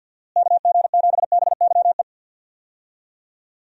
Morse code, heard as a keyed tone, about 52 wpm, 700 Hz. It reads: KG7BYE
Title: USA Call Signs - 6 Characters - Repeated 2x - 50wpm